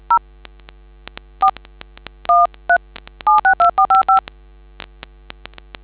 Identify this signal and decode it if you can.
DTMF, heard as telephone touch tones, tones of about 120 ms. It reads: 0 4 13 762455